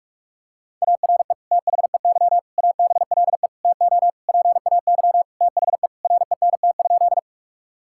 Morse code, heard as a keyed tone, about 46 wpm, 700 Hz. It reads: ARE THEY ABLE TO PAY THE RENT?
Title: Sentences from Top 1000 Words - Review Part 5 - 45wpm